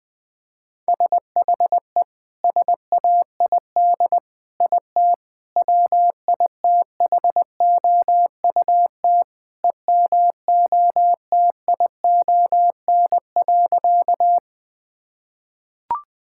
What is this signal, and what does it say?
Morse code, heard as a keyed tone, about 20 wpm, 700 Hz. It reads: SHE SAID IT WITHOUT EMOTION.
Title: Sentences from Top 1600 Words 20wpm